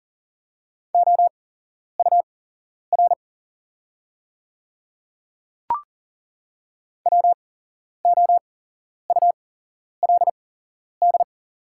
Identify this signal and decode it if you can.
Morse code, heard as a keyed tone, about 40 wpm, spaced as for 11 wpm, 700 Hz. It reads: OUR WOULD
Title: 8x Character Spacing - Top 200 Words - 40wpm